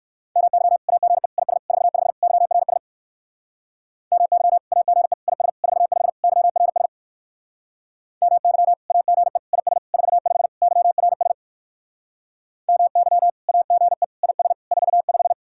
Morse code, heard as a keyed tone, about 64 wpm, 700 Hz. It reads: MY AGE IS 45 YRS MY AGE IS 45 YRS MY AGE IS 45 YRS MY AGE IS 45